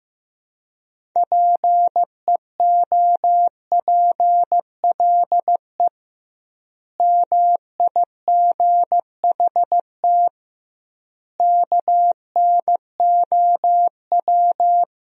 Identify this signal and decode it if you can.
Morse code, heard as a keyed tone, about 15 wpm, 700 Hz. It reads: PEOPLE MIGHT KNOW
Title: 2x Word-Spacing - Sentences from Top 500 Words - 15wpm